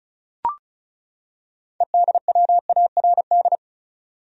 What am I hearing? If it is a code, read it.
Morse code, heard as a keyed tone, about 36 wpm, 700 Hz. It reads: EDWARD
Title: USA Common Names - 35wpm